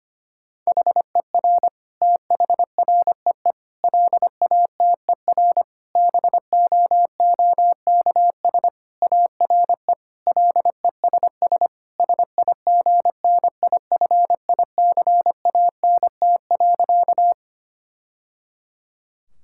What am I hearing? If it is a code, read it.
Morse code, heard as a keyed tone, about 25 wpm, 700 Hz. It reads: HER THREE LATER BOOKS ARE LESS SIGNIFICANT.
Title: Sentences from Top 1000 Words - Review Part 6 - 25wpm